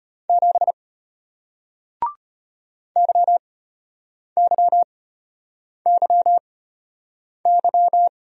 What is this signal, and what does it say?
Morse code, heard as a keyed tone, about 34 wpm, spaced as for 8 wpm, 700 Hz. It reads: 7 YYYY